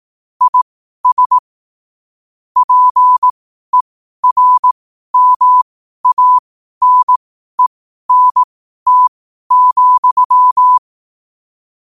Morse code, heard as a keyed tone, about 18 wpm, spaced as for 12 wpm, 1000 Hz. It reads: IS PERMANENT,
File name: qotdfiles / 12wpm